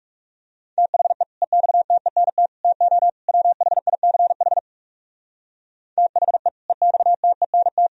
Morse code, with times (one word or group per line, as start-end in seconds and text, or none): 0.8-1.2 THE
1.4-2.5 EXTENT
2.6-3.1 TO
3.3-4.6 WHICH
6.0-6.5 THE
6.7-7.9 EXTENT